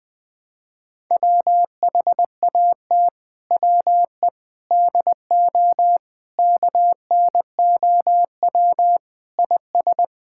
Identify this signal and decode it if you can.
Morse code, heard as a keyed tone, about 21 wpm, 700 Hz. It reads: WHAT WE DO KNOW IS
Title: Sentences with 6-Word Phrases - Spoken Beginning - 20wpm